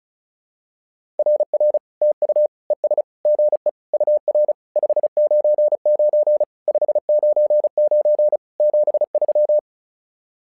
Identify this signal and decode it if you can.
Morse code, heard as a keyed tone, about 36 wpm, 600 Hz. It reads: RR TU ES GE UR 599 599 73
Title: SOTA QSO Practice 35wpm